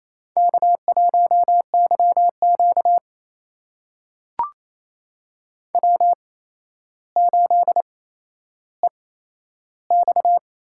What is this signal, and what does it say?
Morse code, heard as a keyed tone, about 28 wpm, spaced as for 12 wpm, 700 Hz. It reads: K1YQ W8EX
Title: USA Call Signs - 4 Characters - 8x Character Spacing - 28wpm